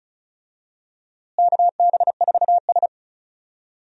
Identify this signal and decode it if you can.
Morse code, heard as a keyed tone, about 34 wpm, 700 Hz. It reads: KB4S
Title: USA Call Signs - 4 Characters - Repeated 2x - 35wpm